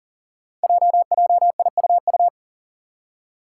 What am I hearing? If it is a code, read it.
Morse code, heard as a keyed tone, about 43 wpm, 700 Hz. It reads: JJIUU
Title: Five Challenging Letter-Number Combinations 40wpm